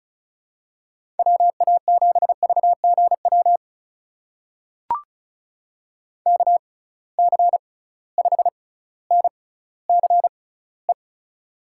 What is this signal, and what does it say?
Morse code, heard as a keyed tone, about 36 wpm, spaced as for 18 wpm, 700 Hz. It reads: WA7VGW KC5NCE
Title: USA Call Signs - 6 Characters - 6x Character Spacing - 35wpm